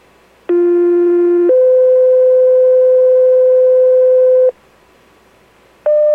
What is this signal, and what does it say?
Two-tone page, heard as a one-second tone, then a three-second tone, about 350 then 510 Hz. It is heard from 0.5 to 4.5 s.